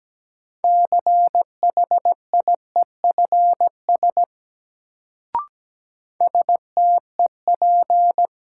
Morse code, heard as a keyed tone, about 17 wpm, 700 Hz. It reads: CHIEFS STEP
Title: Sets of 5 Words Encore - New Word Drill - 17wpm